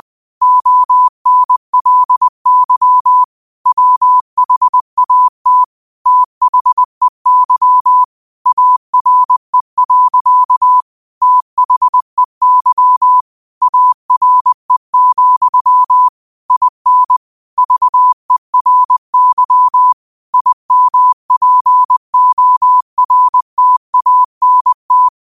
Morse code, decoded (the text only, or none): ONLY WHAT THEY ARE. THEY ARE, IN VERY IMPORTANT